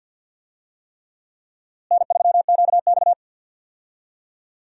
Morse code, heard as a keyed tone, about 53 wpm, 700 Hz. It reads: N3QX
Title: USA Call Signs - 4 Characters - Repeated 3x - 50wpm